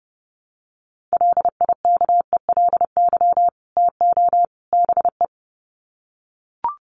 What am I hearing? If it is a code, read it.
Morse code, heard as a keyed tone, about 30 wpm, 700 Hz. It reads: LIKELY TO BE
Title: Sets of 3 Words 30wpm